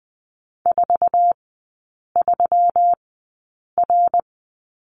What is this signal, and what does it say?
Morse code, heard as a keyed tone, about 20 wpm, spaced as for 8 wpm, 690 Hz. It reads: 43R
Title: 166 Lesson 27e - Number 7 - Three Characters 20wpm